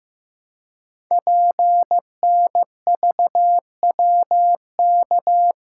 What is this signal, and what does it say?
Morse code, heard as a keyed tone, about 15 wpm, 700 Hz. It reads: PNVWK
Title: Code Groups - 5 Letters - 15wpm